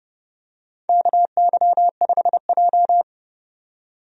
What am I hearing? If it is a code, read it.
Morse code, heard as a keyed tone, about 29 wpm, 700 Hz. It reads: KY5J
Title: USA Call Signs - 4 Characters - 30wpm